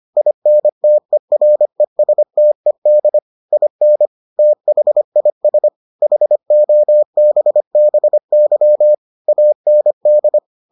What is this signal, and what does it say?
Morse code, heard as a keyed tone, about 25 wpm, 600 Hz. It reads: INTERESTED IN THIS HOBBY AND